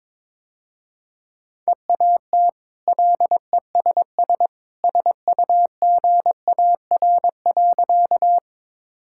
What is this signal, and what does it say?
Morse code, heard as a keyed tone, about 22 wpm, 700 Hz. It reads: EAT LESS SUGAR.
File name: practice / 22wpm / Easy Sentences with Hints - 22wpm